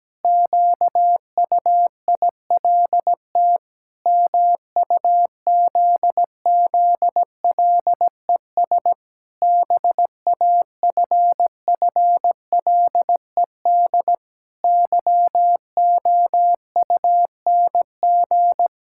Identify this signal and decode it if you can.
Morse code, heard as a keyed tone, about 17 wpm, 700 Hz. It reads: QUILT MUZZLES BAFFLED YOUNG
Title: Pangram Challenge v2 - 17wpm